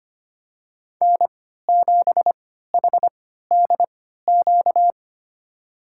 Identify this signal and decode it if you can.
Morse code, heard as a keyed tone, about 25 wpm, spaced as for 14 wpm, 700 Hz. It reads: N7HDQ
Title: USA Call Signs - 3x Character Spacing - 25wpm